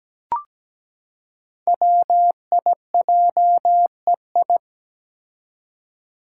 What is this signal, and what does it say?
Morse code, heard as a keyed tone, about 17 wpm, 700 Hz. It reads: WIJEI
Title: Five Challenging Letter-Number Combinations 17wpm